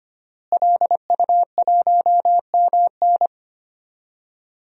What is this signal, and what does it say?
Morse code, heard as a keyed tone, about 25 wpm, 700 Hz. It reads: LU1MN
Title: DXCC-100 Call Signs - 5 Characters - Repeated 2x - 25wpm